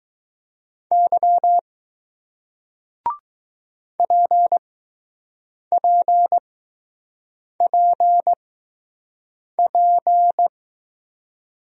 Morse code, heard as a keyed tone, about 20 wpm, spaced as for 6 wpm, 700 Hz. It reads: Y PPPP